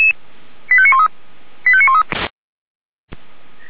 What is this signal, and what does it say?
Five-tone selective call (ZVEI), heard as quick runs of five tones, tones of about 70 ms. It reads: 85913 85913
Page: /464/SDR/ZVEI/89136/2022.05.21/